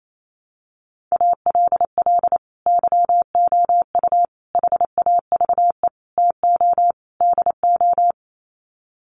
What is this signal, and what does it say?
Morse code, heard as a keyed tone, about 28 wpm, 700 Hz. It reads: ALL YOU HAVE TO DO
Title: Sets of 5 Words 28wpm